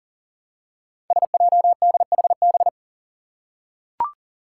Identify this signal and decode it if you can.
Morse code, heard as a keyed tone, about 42 wpm, 700 Hz. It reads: SJDHB
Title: Five Challenging Letter-Number Combinations 40wpm